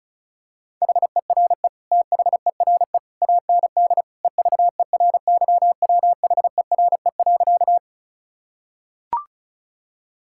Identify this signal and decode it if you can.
Morse code, heard as a keyed tone, about 36 wpm, 700 Hz. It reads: HERE THERE AND EVERYWHERE.
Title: Sentences from Top 1600 Words 35wpm